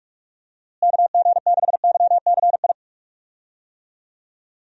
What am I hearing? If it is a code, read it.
Morse code, heard as a keyed tone, about 46 wpm, 700 Hz. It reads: KG6YCI
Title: USA Call Signs - 6 Characters - Repeated 3x - 45wpm